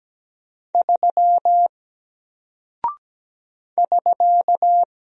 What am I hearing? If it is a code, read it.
Morse code, heard as a keyed tone, about 17 wpm, 700 Hz. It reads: 3 <SK>
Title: Single Character (Letter:Number:Punctuation:Prosign) 17wpm